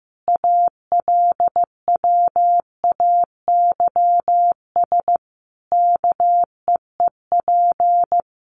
Morse code, heard as a keyed tone, about 15 wpm, 700 Hz. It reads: ALWAYS KEEP
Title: Sentences from Top 1300 Words 15wpm